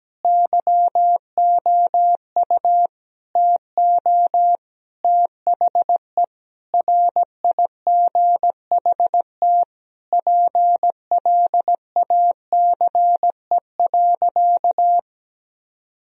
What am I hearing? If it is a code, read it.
Morse code, heard as a keyed tone, about 17 wpm, 700 Hz. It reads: YOU TO THE RIGHT PLACE.
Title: Sentences from Top 1000 Words - Review Part 3 - 17wpm